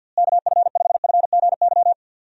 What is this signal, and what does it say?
Morse code, heard as a keyed tone, about 51 wpm, 700 Hz. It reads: KF5FGY
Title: USA Call Signs - 6 Characters - Repeated 2x - 50wpm